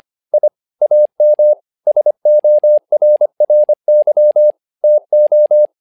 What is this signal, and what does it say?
Morse code, heard as a keyed tone, about 25 wpm, 600 Hz. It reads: I AM SORRY TO
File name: learning center / tower trouble / tower trouble 25wpm